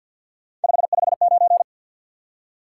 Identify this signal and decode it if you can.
Morse code, heard as a keyed tone, about 54 wpm, 700 Hz. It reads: 559
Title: Top 180 QSO Elements 50wpm